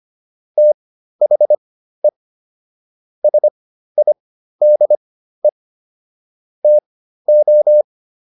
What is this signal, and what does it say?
Morse code, heard as a keyed tone, about 25 wpm, spaced as for 13 wpm, 600 Hz. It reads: THE SIDE TO